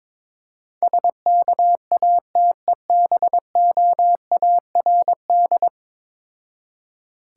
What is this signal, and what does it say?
Morse code, heard as a keyed tone, about 22 wpm, 700 Hz. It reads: SKATEBOARD
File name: Difficult Words with Hints - 22wpm